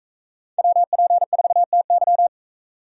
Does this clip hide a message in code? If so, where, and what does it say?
Yes: Morse code from 0.6 to 2.3 s, heard as a keyed tone, about 41 wpm, 700 Hz. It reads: WP4TY